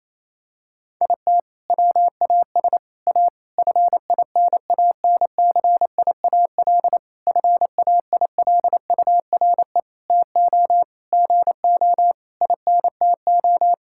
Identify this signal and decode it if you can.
Morse code, heard as a keyed tone, about 28 wpm, 700 Hz. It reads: IT WAS A FINANCIAL FAILURE TO GO INTO
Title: Sentences from Top 1000 Words - Review Part 6 - 28wpm